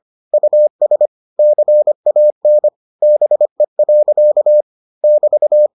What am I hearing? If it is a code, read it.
Morse code, heard as a keyed tone, about 25 wpm, 600 Hz. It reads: US CAN BE. <BT>